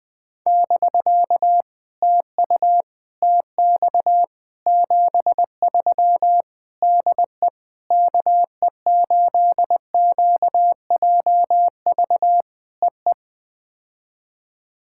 Morse code, heard as a keyed tone, about 18 wpm, 700 Hz. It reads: <BK> TU TX 73 DE KE8QJV EE
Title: POTA QSO Practice - 20wpm